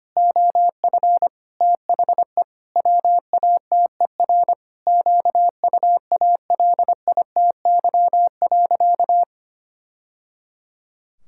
Morse code, heard as a keyed tone, about 25 wpm, 700 Hz. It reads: OF THE WATER QUALITY.
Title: Sentences from Top 1000 Words - Review Part 5 - 25wpm